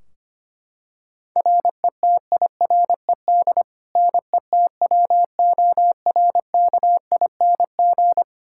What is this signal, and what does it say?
Morse code, heard as a keyed tone, about 25 wpm, 700 Hz. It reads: RETIRED NETWORKING
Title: K1USN SST QSO Practice 25wpm